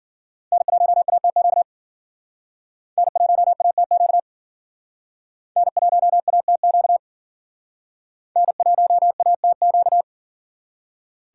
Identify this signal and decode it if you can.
Morse code, heard as a keyed tone, about 53 wpm, 700 Hz. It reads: N1ATQ N1ATQ N1ATQ N1ATQ